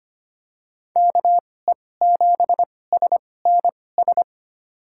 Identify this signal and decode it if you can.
Morse code, heard as a keyed tone, about 25 wpm, spaced as for 18 wpm, 700 Hz. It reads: KE7SNS
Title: USA Call Signs - 6 Characters - 2x Character Spacing - 25wpm